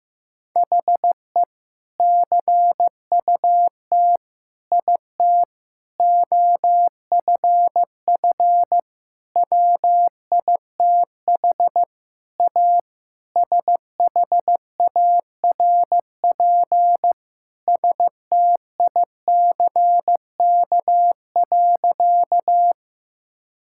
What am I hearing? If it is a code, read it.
Morse code, heard as a keyed tone, about 15 wpm, 700 Hz. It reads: HE CUT IT OFF WITH A SHARP STICK.
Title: Sentences from Top 1600 Words 15wpm